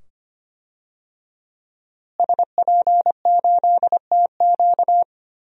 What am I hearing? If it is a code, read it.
Morse code, heard as a keyed tone, about 25 wpm, 700 Hz. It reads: SP8TQ
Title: DXCC-100 Call Signs - 5 Characters - Repeated 2x - 25wpm